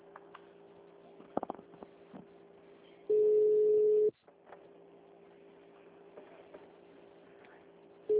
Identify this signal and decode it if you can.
Ringback tone (Europe, 425 Hz).
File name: out- -2004- - - .72